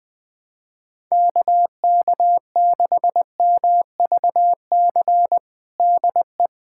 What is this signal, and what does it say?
Morse code, heard as a keyed tone, about 20 wpm, 700 Hz. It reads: KK6MVC DE
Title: USA Callsign DE Callsign - 6 Characters - 20wpm